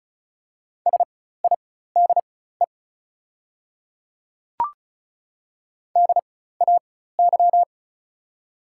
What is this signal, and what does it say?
Morse code, heard as a keyed tone, about 34 wpm, spaced as for 16 wpm, 690 Hz. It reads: SIDE DAY
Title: 4x Character Spacing - Top 100 Words - 35wpm